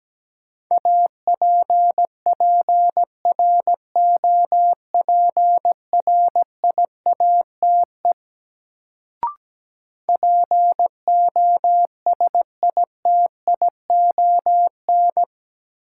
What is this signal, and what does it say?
Morse code, heard as a keyed tone, about 17 wpm, 700 Hz. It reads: APPROPRIATE POSITION
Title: Sentences from Top 900 Words - New Word Drill - 17wpm